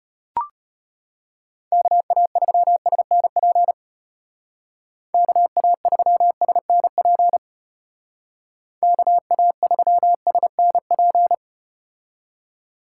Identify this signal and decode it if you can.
Morse code, heard as a keyed tone, about 33 wpm, 700 Hz. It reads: KA3SNP KA3SNP KA3SNP